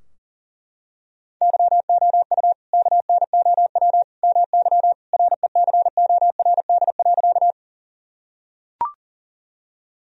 Morse code, heard as a keyed tone, about 37 wpm, 700 Hz. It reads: YOU KNOW MY RECORD.
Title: Sentences from Top 1000 Words - Review Part 3 - 40wpm